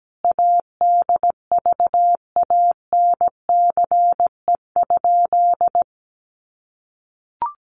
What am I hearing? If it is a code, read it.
Morse code, heard as a keyed tone, about 17 wpm, 700 Hz. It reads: ADVANCE?
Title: Sentences from Top 1400 Words 17wpm